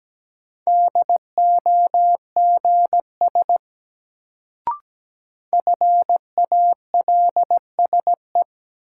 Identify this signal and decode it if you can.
Morse code, heard as a keyed tone, about 17 wpm, 700 Hz. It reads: DOGS FALSE
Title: Sentences from Top 1600 Words - New Word Drill - 17wpm